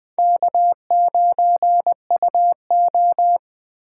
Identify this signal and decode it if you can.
Morse code, heard as a keyed tone, about 20 wpm, 700 Hz. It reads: K9UO